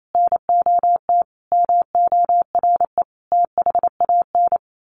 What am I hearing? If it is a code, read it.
Morse code, heard as a keyed tone, about 28 wpm, 700 Hz. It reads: NOT MORE THAN
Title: A Sound Of Thunder 28wpm